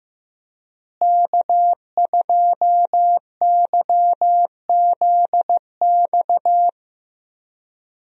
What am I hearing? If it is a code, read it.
Morse code, heard as a keyed tone, about 15 wpm, 700 Hz. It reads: K2YZX
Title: USA Call Signs - 5 Characters - Repeated 3x - 15wpm